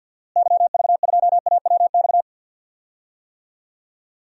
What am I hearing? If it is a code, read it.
Morse code, heard as a keyed tone, about 45 wpm, 700 Hz. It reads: YV2AWX
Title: DXCC-100 Call Signs - 6 Characters - Repeated 2x - 50wpm